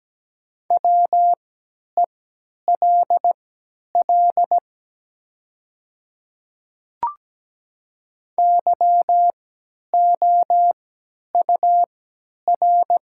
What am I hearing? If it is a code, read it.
Morse code, heard as a keyed tone, about 17 wpm, spaced as for 10 wpm, 700 Hz. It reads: WELL YOUR
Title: 3x Character Spacing - Top 200 Words - 17wpm